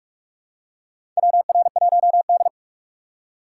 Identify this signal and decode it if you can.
Morse code, heard as a keyed tone, about 46 wpm, 700 Hz. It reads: WR1D
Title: USA Call Signs - 2x Character Spacing - 45wpm